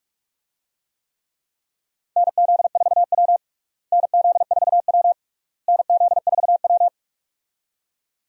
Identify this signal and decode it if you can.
Morse code, heard as a keyed tone, about 46 wpm, 700 Hz. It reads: NZ4W NZ4W NZ4W
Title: USA Call Signs - 4 Characters - Repeated 3x - 45wpm